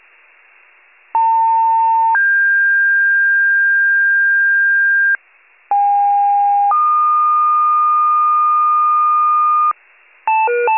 Two-tone page, heard as a one-second tone, then a three-second tone, about 900 then 1600 Hz, about 810 then 1220 Hz, with noise. It is heard from 1.1 to 9.7 s.